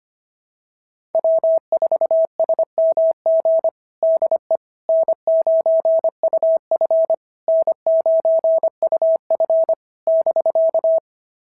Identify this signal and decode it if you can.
Morse code, heard as a keyed tone, about 25 wpm, 660 Hz. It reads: W4SMG DE N9UF N9UF <BK>
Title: SOTA QSO Practice 25wpm